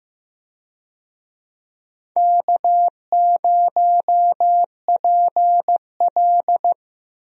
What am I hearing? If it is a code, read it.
Morse code, heard as a keyed tone, about 14 wpm, 700 Hz. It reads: K0PL
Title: USA Call Signs - 4 Characters - Repeated 3x - 15wpm